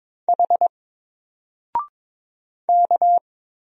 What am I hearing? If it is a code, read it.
Morse code, heard as a keyed tone, about 22 wpm, 700 Hz. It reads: H K
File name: Single-Letter-Number-22wpm